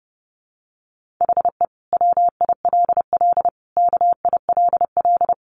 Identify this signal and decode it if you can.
Morse code, heard as a keyed tone, about 30 wpm, 700 Hz. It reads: HE WILL KILL
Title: Sentences from Top 1400 Words 30wpm